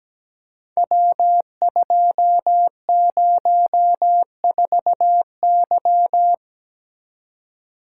Morse code, heard as a keyed tone, about 17 wpm, 700 Hz. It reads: W204Y
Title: Code Groups - 5 Letter:Numbers - 17wpm